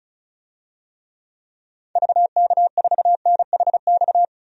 Morse code, heard as a keyed tone, about 36 wpm, 700 Hz. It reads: VK4NHX
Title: DXCC-100 Call Signs - 6 Characters - 35wpm